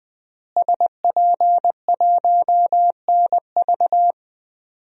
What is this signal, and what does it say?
Morse code, heard as a keyed tone, about 20 wpm, 700 Hz. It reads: SP1NV
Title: DXCC-100 Call Signs - 5 Characters - Repeated 3x - 20wpm